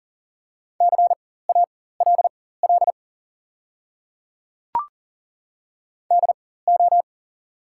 Morse code, heard as a keyed tone, about 40 wpm, spaced as for 19 wpm, 700 Hz. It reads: CALL DO
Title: 4x Character Spacing - Top 100 Words - 40wpm